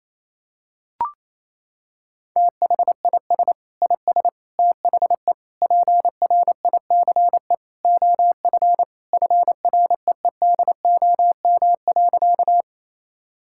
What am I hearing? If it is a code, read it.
Morse code, heard as a keyed tone, about 28 wpm, 700 Hz. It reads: THIS IS THE PRICE OF FREEDOM.
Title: Sentences from Top 1000 Words - Review Part 5 - 28wpm